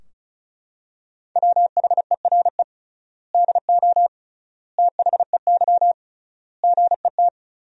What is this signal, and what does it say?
Morse code, heard as a keyed tone, about 36 wpm, 700 Hz. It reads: WHERE DO THEY GET
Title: 3x Word-Spacing - Sentences from Top 200 Words - 35wpm